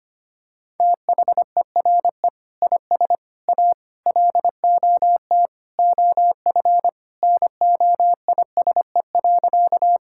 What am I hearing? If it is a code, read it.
Morse code, heard as a keyed tone, about 25 wpm, 700 Hz. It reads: THERE IS A LOT OF NOISE.
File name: practice / 25wpm / Sentences with 2-Word Phrases - 25wpm